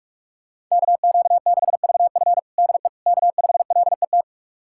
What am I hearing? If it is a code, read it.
Morse code, heard as a keyed tone, about 44 wpm, 700 Hz. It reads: KQ6VF DE K5LET
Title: USA Callsign DE Callsign - 5 Characters - 45wpm